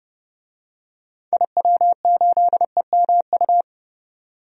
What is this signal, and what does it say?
Morse code, heard as a keyed tone, about 30 wpm, 700 Hz. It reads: IW8EMU
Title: DXCC-100 Call Signs - 6 Characters - 30wpm